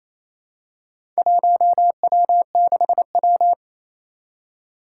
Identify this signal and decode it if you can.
Morse code, heard as a keyed tone, about 27 wpm, 700 Hz. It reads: 1W6W
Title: Four Challenging Letter-Number Combinations 28wpm